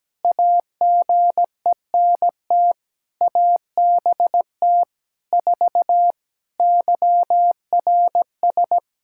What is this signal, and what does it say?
Morse code, heard as a keyed tone, about 17 wpm, 700 Hz. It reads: AGENT ABT 4 YRS
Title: K1USN SST QSO Practice 17wpm